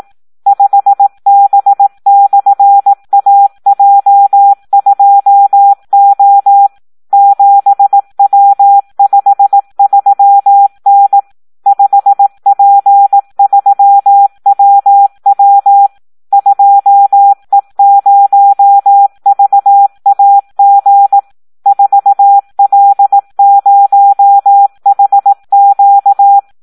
Morse code, heard as a keyed tone, about 18 wpm, 800 Hz. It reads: 5B/AJ2O 7W53N 5P3WW 2E0VAG 4L0HQ
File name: Audio practica CW / SD 1 18